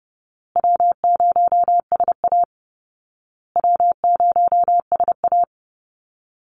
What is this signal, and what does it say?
Morse code, heard as a keyed tone, about 30 wpm, 700 Hz. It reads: W0SA W0SA
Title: Easy Call Signs Repeated 3x - 30wpm